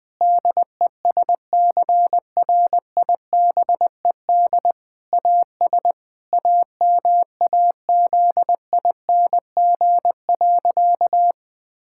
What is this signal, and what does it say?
Morse code, heard as a keyed tone, about 20 wpm, 700 Hz. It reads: DESCRIBED AS AMAZING.
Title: Sentences with 6-Word Phrases - Spoken Beginning - 20wpm